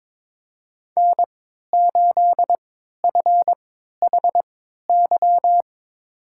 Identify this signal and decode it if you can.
Morse code, heard as a keyed tone, about 22 wpm, spaced as for 12 wpm, 700 Hz. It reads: N8FHY
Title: USA Call Signs - 5 Characters - 3x Character Spacing - 22wpm